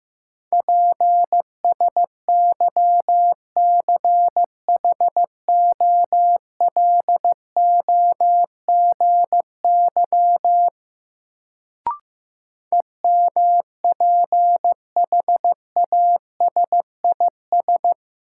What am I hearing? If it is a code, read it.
Morse code, heard as a keyed tone, about 15 wpm, 700 Hz. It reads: PSYCHOLOGY EMPHASIS
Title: Sentences from Top 1400 Words - New Word Drill Alt - 15wpm